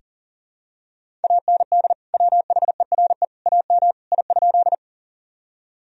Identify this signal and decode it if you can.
Morse code, heard as a keyed tone, about 40 wpm, 700 Hz. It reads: AND WHERE AM I?
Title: Sentences from Top 1000 Words - Review Part 2 - 40wpm